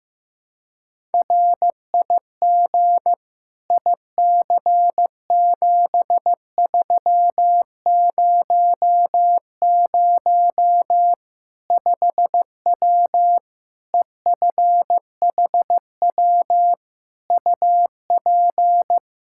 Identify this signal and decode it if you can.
Morse code, heard as a keyed tone, about 15 wpm, 700 Hz. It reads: RIG IC7300 5W EFHW UP
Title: Practice QSO - Exchange 2 - Info - 15wpm